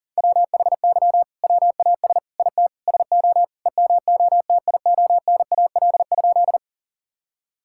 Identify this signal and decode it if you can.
Morse code, heard as a keyed tone, about 40 wpm, 700 Hz. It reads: WHY WAS IT SO EMOTIONAL?
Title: Sentences from Top 1000 Words - Review Part 5 - 40wpm